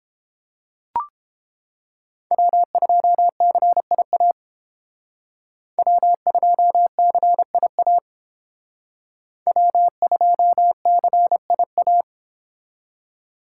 Morse code, heard as a keyed tone, about 31 wpm, 700 Hz. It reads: W2CIA W2CIA W2CIA